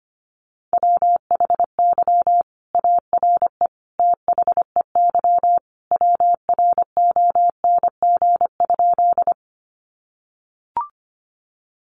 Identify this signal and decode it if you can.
Morse code, heard as a keyed tone, about 25 wpm, 700 Hz. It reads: WHY ARE THEY WRONG?
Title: Sentences from Top 900 Words 25wpm